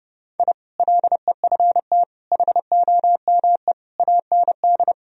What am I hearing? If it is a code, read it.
Morse code, heard as a keyed tone, about 30 wpm, 700 Hz. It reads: I LEFT HOME AND
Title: Sentences from Top 1000 Words - Review Part 3 - 30wpm